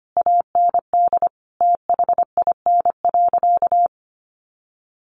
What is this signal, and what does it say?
Morse code, heard as a keyed tone, about 25 wpm, 700 Hz. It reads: AND THIN.
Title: Sentences from Top 1200 Words 25wpm